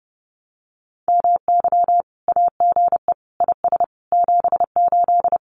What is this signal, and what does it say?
Morse code, heard as a keyed tone, about 30 wpm, 700 Hz. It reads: MY AGE IS 78
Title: Common QSO Phrases Encore 30wpm